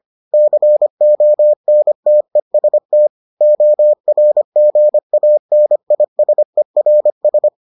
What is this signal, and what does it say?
Morse code, heard as a keyed tone, about 25 wpm, 600 Hz. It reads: CONTEST ORGANISERS